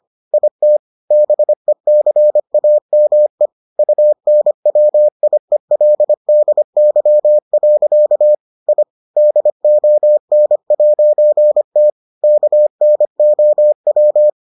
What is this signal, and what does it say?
Morse code, heard as a keyed tone, about 25 wpm, 600 Hz. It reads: IT BECAME UNWIELDY. I DON'T KNOW